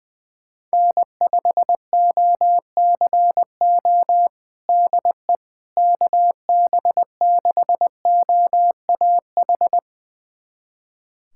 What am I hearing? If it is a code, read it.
Morse code, heard as a keyed tone, about 20 wpm, 700 Hz. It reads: N5OCO DE KB6OAH
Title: USA Callsign DE Callsign 20wpm